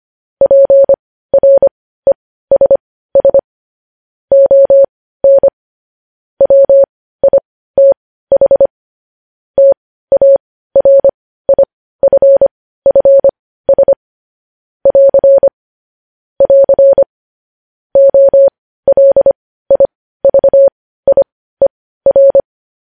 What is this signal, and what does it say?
Morse code, heard as a keyed tone, about 25 wpm, spaced as for 15 wpm, 550 Hz. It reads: PRESS ON WITH TARIFFS <AR> <AR> OLIVIER
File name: podcast / News headlines-1x-25x15